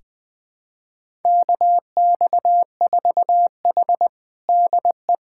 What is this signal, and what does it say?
Morse code, heard as a keyed tone, about 20 wpm, 700 Hz. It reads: KX4H DE
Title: USA Callsign DE Callsign 20wpm